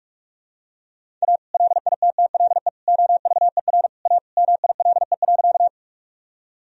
Morse code, heard as a keyed tone, about 46 wpm, 700 Hz. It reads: A LITTLE OVER A MILE.
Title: Sentences from Top 1000 Words - Review Part 1 - 45wpm